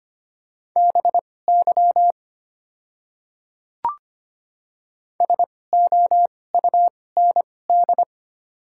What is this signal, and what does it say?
Morse code, heard as a keyed tone, about 25 wpm, spaced as for 18 wpm, 690 Hz. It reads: BY SOUND
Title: 2x Character Spacing - Top 300 Words - 25wpm